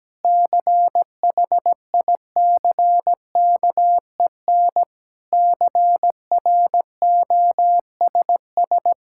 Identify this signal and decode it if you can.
Morse code, heard as a keyed tone, about 17 wpm, 700 Hz. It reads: CHICKEN CROSS
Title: Sentences from Top 1000 Words - Review Part 5 - 17wpm